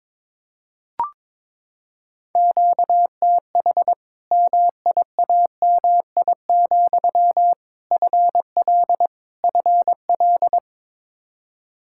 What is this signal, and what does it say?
Morse code, heard as a keyed tone, about 22 wpm, 700 Hz. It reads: QTH MIAMI, FL FL
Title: Practice QSO - Exchange 1 - Beginning - 22wpm